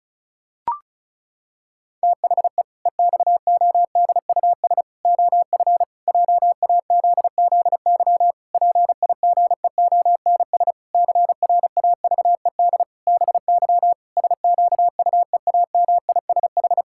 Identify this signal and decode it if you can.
Morse code, heard as a keyed tone, about 36 wpm, 700 Hz. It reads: THE EXODUS OF JAZZY PIGEONS CRAVED BY SQUEAMISH
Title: Pangram Challenge v2 - 35wpm